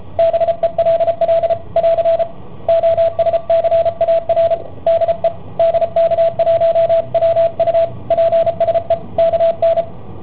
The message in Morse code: BELL <AR> OSCAR DE DK1WU PSE KN